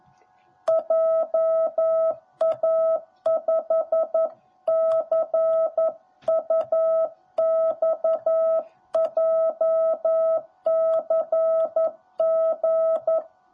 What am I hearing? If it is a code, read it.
Morse code, heard as a keyed tone, about 11 wpm, 650 Hz. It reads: JA5CUX JCG
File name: jh8cbh / 2021cw / cw10b